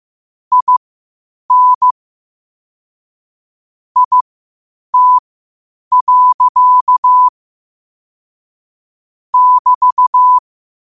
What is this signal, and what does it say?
Morse code, heard as a keyed tone, about 15 wpm, spaced as for 8 wpm, 1000 Hz. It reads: IN IT. <BT>
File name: qotdfiles / 8wpm